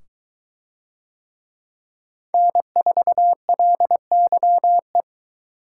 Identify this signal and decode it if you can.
Morse code, heard as a keyed tone, about 22 wpm, 700 Hz. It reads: N4LYE